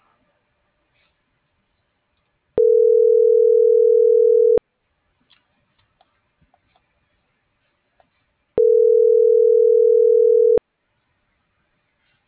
Ringback tone (North America), 440 + 480 Hz.